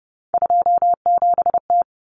Morse code, heard as a keyed tone, about 30 wpm, 700 Hz. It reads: 27T
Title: Common Cut Numbers - 3-Digit 30wpm